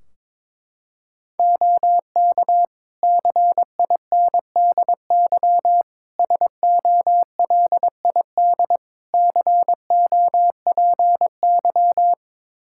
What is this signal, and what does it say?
Morse code, heard as a keyed tone, about 22 wpm, 700 Hz. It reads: OK CINDY SOLID COPY
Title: K1USN SST QSO Practice 22wpm